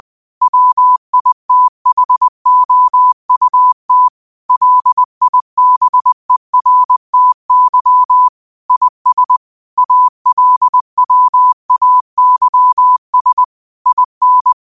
Morse code, heard as a keyed tone, about 20 wpm, 1000 Hz. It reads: WITHOUT LIBERTY IS ALWAYS IN
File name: qotdfiles / 20wpm